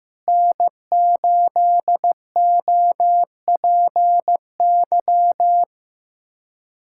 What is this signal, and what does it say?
Morse code, heard as a keyed tone, about 15 wpm, 700 Hz. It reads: N8OPY
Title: USA Callsign DE Callsign - 5 Characters - 15wpm